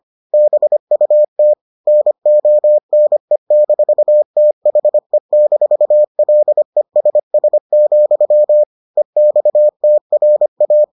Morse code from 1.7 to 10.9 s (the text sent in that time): NONE-THE-LESS, EXTRA